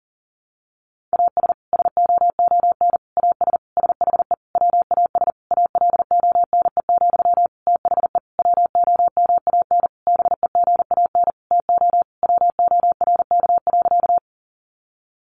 Morse code, heard as a keyed tone, about 40 wpm, 700 Hz. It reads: AS SOON AS SHE WAS ALONE, THE WOMAN BEGAN TO WORK.
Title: Sentences from Top 1100 Words 40wpm